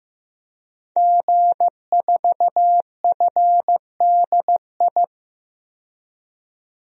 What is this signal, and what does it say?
Morse code, heard as a keyed tone, about 16 wpm, 700 Hz. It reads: G4FDI
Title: DXCC-100 Call Signs - 5 Characters - 15wpm